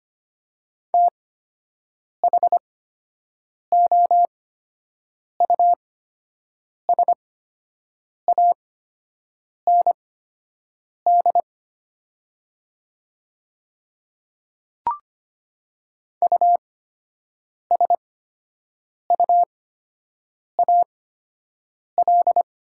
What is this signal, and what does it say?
Morse code, heard as a keyed tone, about 25 wpm, spaced as for 7 wpm, 700 Hz. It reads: THOUSAND USUAL
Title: 8x Character Spacing - Top 500 Words - 25wpm